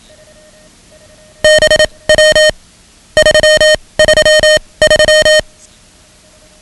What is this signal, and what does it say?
Morse code, heard as a keyed tone, about 27 wpm, spaced as for 21 wpm, 620 Hz. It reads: BW 333
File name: ham radio / CodeQuick / Code Quick 10